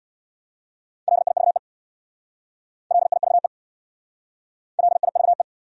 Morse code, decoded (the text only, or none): LIVE LIVE LIVE